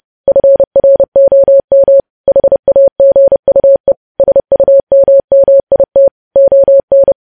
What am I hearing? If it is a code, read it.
Morse code, heard as a keyed tone, about 30 wpm, 550 Hz. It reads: FROM HAGUE SUMMIT ON
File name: podcast / News headlines-1x-30x30